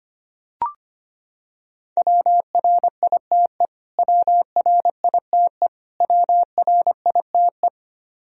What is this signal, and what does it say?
Morse code, heard as a keyed tone, about 25 wpm, 700 Hz. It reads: WRITE WRITE WRITE
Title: Top 100 Words Repeated 25wpm